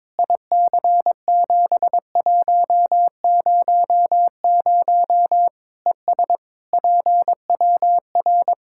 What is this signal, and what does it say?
Morse code, heard as a keyed tone, about 22 wpm, 700 Hz. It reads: IC7100 ES PWR
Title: Practice QSO - Exchange 2 - Info - 22wpm